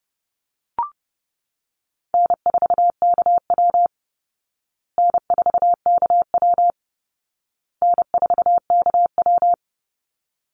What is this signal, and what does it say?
Morse code, heard as a keyed tone, about 28 wpm, 700 Hz. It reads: N4KW N4KW N4KW
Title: Easy Call Signs Repeated 3x - 30wpm